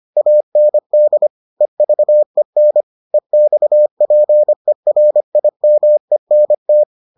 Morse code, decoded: AND EVEN EXPERIMENT

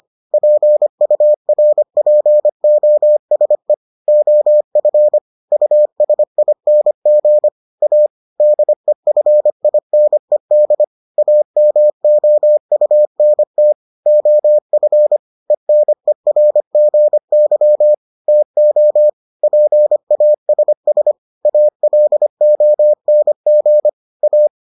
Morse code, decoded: PURPOSE OF USING A DEFINED AMOUNT OF ENERGY TO PASS ALONG A

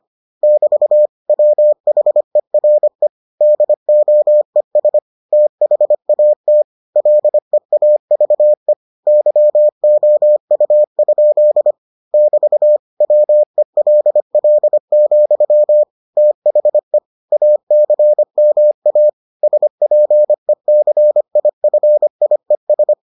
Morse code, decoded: <BT> WHERE DOES THAT LEAVE YOU? <BT> WELL, THE ACMA SPECIFIES

